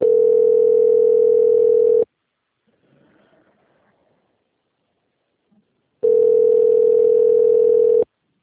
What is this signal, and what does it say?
Ringback tone (North America), 440 + 480 Hz.